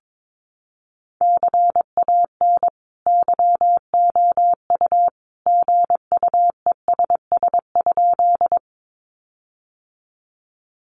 Morse code, heard as a keyed tone, about 22 wpm, 700 Hz. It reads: CAN YOU GUESS?
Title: Sentences from Top 1500 Words 22wpm